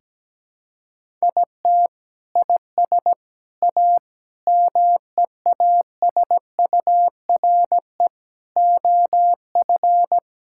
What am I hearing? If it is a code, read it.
Morse code, heard as a keyed tone, about 17 wpm, 700 Hz. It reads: IT IS A MEASURE OF
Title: Sentences from Top 1600 Words 17wpm